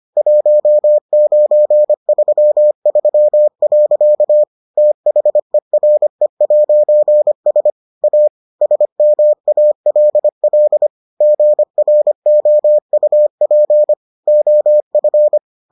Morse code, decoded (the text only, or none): 1933. THERE'S A SMALL GROUP OF